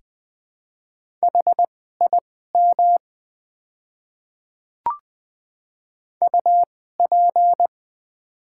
Morse code, heard as a keyed tone, about 20 wpm, spaced as for 15 wpm, 700 Hz. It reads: HIM UP
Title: 2x Character Spacing - Top 100 Words - 20wpm